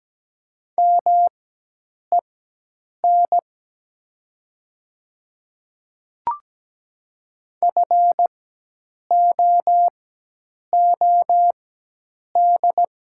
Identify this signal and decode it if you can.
Morse code, heard as a keyed tone, about 17 wpm, spaced as for 8 wpm, 700 Hz. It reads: MEN FOOD
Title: 4x Character Spacing - Top 300 Words - 17wpm